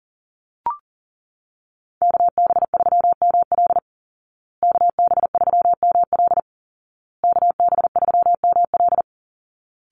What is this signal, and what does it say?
Morse code, heard as a keyed tone, about 40 wpm, 700 Hz. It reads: KB3ML KB3ML KB3ML
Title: Call Signs Repeated 2x - 40wpm